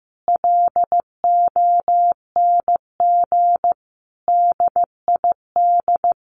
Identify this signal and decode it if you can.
Morse code, heard as a keyed tone, about 15 wpm, 700 Hz. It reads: LONG DID
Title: Sentences from Top 1500 Words 15wpm